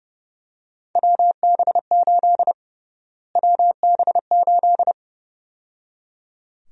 Morse code, heard as a keyed tone, about 30 wpm, 700 Hz. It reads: WB8 WB8
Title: USA Call Sign Prefixes - Repeated 2x - 30wpm